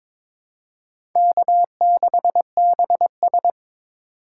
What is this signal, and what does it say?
Morse code, heard as a keyed tone, about 22 wpm, 700 Hz. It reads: K6BS
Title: USA Call Signs - 4 Characters - 4x Character Spacing - 22wpm